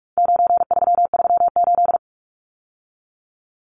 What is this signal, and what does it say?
Morse code, heard as a keyed tone, about 44 wpm, 700 Hz. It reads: 9338
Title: Numbers - 4-Digit 45wpm